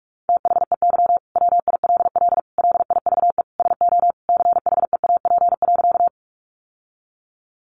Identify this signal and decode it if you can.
Morse code, heard as a keyed tone, about 44 wpm, 700 Hz. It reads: THEY WILL LIVE SO CHEAP.
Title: Sentences from Top 1200 Words 45wpm